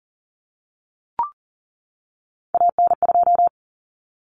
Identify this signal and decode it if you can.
Morse code, heard as a keyed tone, about 40 wpm, 700 Hz. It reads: AN2
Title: Common Cut Numbers - 3-Digit 40wpm